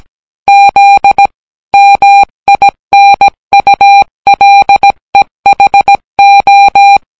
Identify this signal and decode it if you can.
Morse code, heard as a keyed tone, about 17 wpm, 800 Hz. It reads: Z MINULEHO